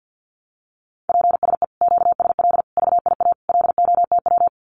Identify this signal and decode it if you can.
Morse code, heard as a keyed tone, about 50 wpm, 700 Hz. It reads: PSE QSL VIA LOTW
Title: Common QSO Phrases Encore 50wpm